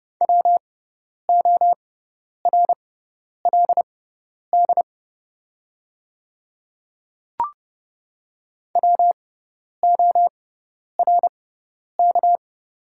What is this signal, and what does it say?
Morse code, heard as a keyed tone, about 30 wpm, spaced as for 10 wpm, 690 Hz. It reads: WORLD WORK
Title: 6x Character Spacing - Top 200 Words - 30wpm